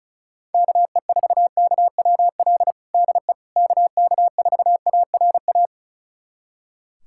Morse code, heard as a keyed tone, about 36 wpm, 700 Hz. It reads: KE4KWL DE KK4ARA
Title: USA Callsign DE Callsign - 6 Characters - 35wpm